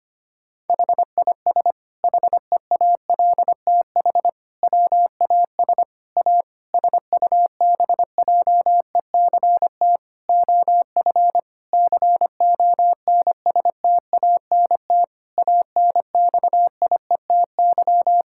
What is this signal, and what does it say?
Morse code, heard as a keyed tone, about 25 wpm, 700 Hz. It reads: HIS HEALTH WAS A SUBJECT OF CONSTANT ANXIETY
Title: Sentences from Top 1600 Words 25wpm